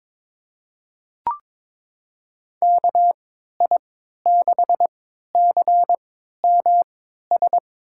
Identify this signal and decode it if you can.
Morse code, heard as a keyed tone, about 21 wpm, spaced as for 12 wpm, 700 Hz. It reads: KI6CMS
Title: USA Call Signs - 3x Character Spacing - 22wpm